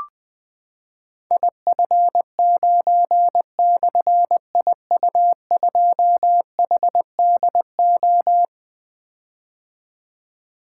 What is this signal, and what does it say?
Morse code, heard as a keyed tone, about 20 wpm, 700 Hz. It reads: IF9/IU2HDO
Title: Difficult Call Signs 20wpm